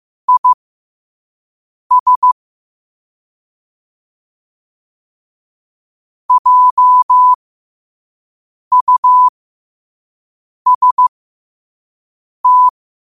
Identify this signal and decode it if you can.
Morse code, heard as a keyed tone, about 15 wpm, spaced as for 5 wpm, 1000 Hz. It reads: IS JUST
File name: qotdfiles / 5wpm